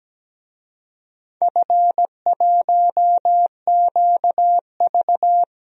Morse code, heard as a keyed tone, about 17 wpm, 700 Hz. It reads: F1QV
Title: DXCC-100 Call Signs - 4 Characters - Repeated 3x - 17wpm